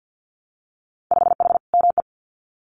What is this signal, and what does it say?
Morse code, heard as a keyed tone, about 50 wpm, 700 Hz. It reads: 5H ME
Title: Winter Field Day Exchanges Encore 50wpm